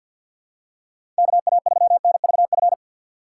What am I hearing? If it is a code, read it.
Morse code, heard as a keyed tone, about 52 wpm, 690 Hz. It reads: KA3NVF
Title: USA Call Signs - 6 Characters - 3x Character Spacing - 50wpm